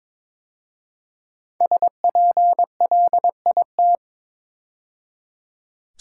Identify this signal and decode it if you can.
Morse code, heard as a keyed tone, about 22 wpm, 700 Hz. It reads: SPLIT
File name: 5 Letter Words - First 3 Letters Given - 22wpm